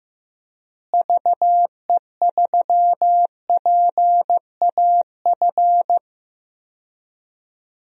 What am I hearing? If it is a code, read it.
Morse code, heard as a keyed tone, about 15 wpm, 700 Hz. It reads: VE3PAF
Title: DXCC-100 Call Signs - 6 Characters - Repeated 3x - 15wpm